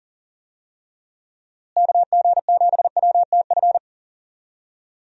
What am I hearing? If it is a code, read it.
Morse code, heard as a keyed tone, about 40 wpm, 700 Hz. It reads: KG7WTF
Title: USA Call Signs - 6 Characters - Repeated 2x - 40wpm